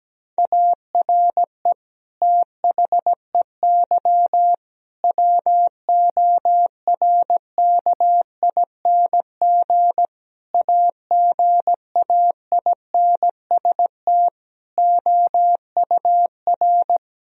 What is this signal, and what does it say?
Morse code, heard as a keyed tone, about 17 wpm, 700 Hz. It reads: ARE THEY WORKING AGAINST OUR
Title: Sentences from Top 1000 Words - Review Part 4 - 17wpm